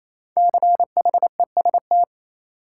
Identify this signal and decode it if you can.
Morse code, heard as a keyed tone, about 28 wpm, 700 Hz. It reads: CHEST
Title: Five Letter Words - 28wpm